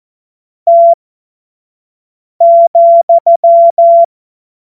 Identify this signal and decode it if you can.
Morse code, heard as a keyed tone, about 14 wpm, spaced as for 5 wpm, 680 Hz. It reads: T,
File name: Morsekurs / Witze / Witz 20. Juli 2021, 25 F70